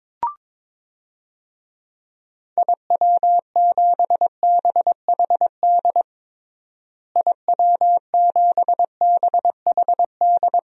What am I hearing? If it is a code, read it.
Morse code, heard as a keyed tone, about 22 wpm, 700 Hz. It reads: IW7BHD IW7BHD
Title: DXCC-100 Call Signs - 6 Characters - Repeated 3x - 22wpm